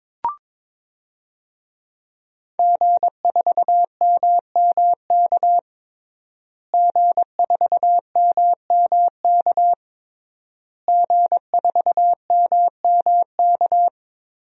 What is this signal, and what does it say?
Morse code, heard as a keyed tone, about 22 wpm, 700 Hz. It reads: G4MMK G4MMK G4MMK
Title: DXCC-100 Call Signs - 5 Characters - Repeated 3x - 22wpm